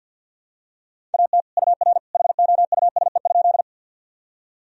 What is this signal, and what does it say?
Morse code, heard as a keyed tone, about 52 wpm, 700 Hz. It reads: AT UR HOUSE?
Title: Sentences from Top 200 Words and Abbreviations 50wpm